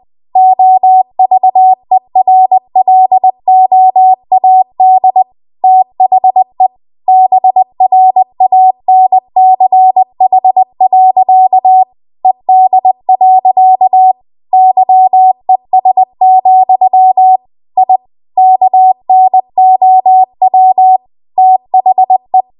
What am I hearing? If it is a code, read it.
Morse code, heard as a keyed tone, about 20 wpm, 750 Hz. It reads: OVERLOAD THE BRANCH. ED. YES, I KNOW THE
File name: Archive / 20 WPM / 20WPM